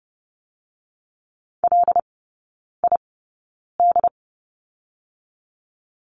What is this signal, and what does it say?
Morse code, heard as a keyed tone, about 30 wpm, spaced as for 9 wpm, 700 Hz. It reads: LID
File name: Lesson 19e - Letter w - Three Characters 30wpm